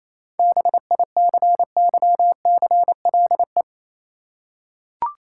Morse code, heard as a keyed tone, about 28 wpm, 690 Hz. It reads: BICYCLE